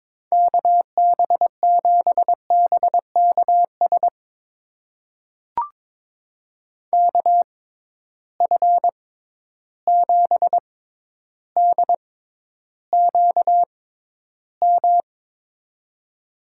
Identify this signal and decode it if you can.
Morse code, heard as a keyed tone, about 23 wpm, spaced as for 11 wpm, 700 Hz. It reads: KB7BKS KF7DQM
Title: USA Call Signs - 6 Characters - 6x Character Spacing - 22wpm